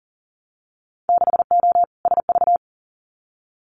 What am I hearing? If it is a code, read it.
Morse code, heard as a keyed tone, about 40 wpm, 700 Hz. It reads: 6O SV
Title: Winter Field Day Exchanges Encore 40wpm